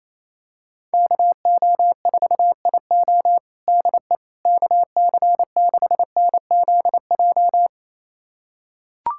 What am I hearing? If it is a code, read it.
Morse code, heard as a keyed tone, about 27 wpm, 700 Hz. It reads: KO4IO DE KC6NZJ
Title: USA Callsign DE Callsign 28wpm